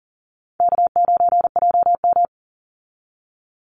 Morse code, heard as a keyed tone, about 40 wpm, 700 Hz. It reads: K9JM
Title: Easy Call Signs - 40wpm